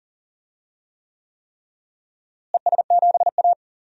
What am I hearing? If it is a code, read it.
Morse code, heard as a keyed tone, about 40 wpm, 700 Hz. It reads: ES7A
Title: DXCC-100 Call Signs - 4 Characters - Repeated 3x - 40wpm